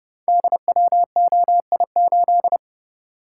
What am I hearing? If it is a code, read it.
Morse code, heard as a keyed tone, about 28 wpm, 700 Hz. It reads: DWOI8
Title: Code Groups - 5 Letter:Numbers - 30wpm